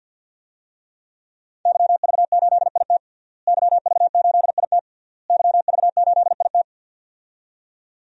Morse code, heard as a keyed tone, about 52 wpm, 700 Hz. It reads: YV8IT YV8IT YV8IT
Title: DXCC-100 Call Signs - 5 Characters - Repeated 3x - 50wpm